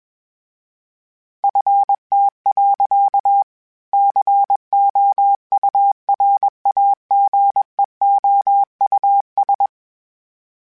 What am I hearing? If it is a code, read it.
Morse code, heard as a keyed tone, about 21 wpm, 800 Hz. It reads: FT. COURAGEOUS
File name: fromSMs / FromSMs mp3s / Cities@20